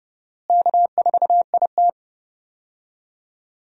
Morse code, heard as a keyed tone, about 32 wpm, 700 Hz. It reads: K4IT
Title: USA Call Signs - 4 Characters - Repeated 3x - 30wpm